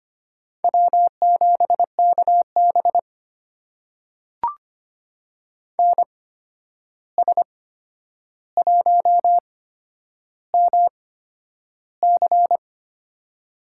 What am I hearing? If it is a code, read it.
Morse code, heard as a keyed tone, about 25 wpm, spaced as for 10 wpm, 700 Hz. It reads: W7KB NS1MC
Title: USA Call Signs - 8x Character Spacing - 25wpm